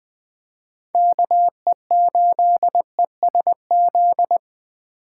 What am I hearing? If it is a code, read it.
Morse code, heard as a keyed tone, about 20 wpm, 700 Hz. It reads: KE8ESZ
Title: USA Call Signs - 6 Characters - 4x Character Spacing - 20wpm